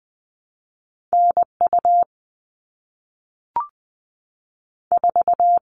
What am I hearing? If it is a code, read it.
Morse code, heard as a keyed tone, about 20 wpm, 700 Hz. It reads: NU 4
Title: All Cut Numbers - 2-Digit 20wpm